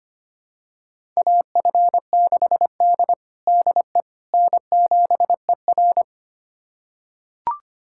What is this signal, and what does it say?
Morse code, heard as a keyed tone, about 25 wpm, 700 Hz. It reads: AF6D DE N7ER
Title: USA Callsign DE Callsign - 4 Characters - 25wpm